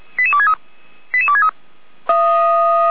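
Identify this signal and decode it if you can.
Five-tone selective call (ZVEI), heard as quick runs of five tones, tones of about 70 ms. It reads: 88263
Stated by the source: ZVEI five-tone selective call received via SDR, fire brigade Feuerwehr Orlamünde.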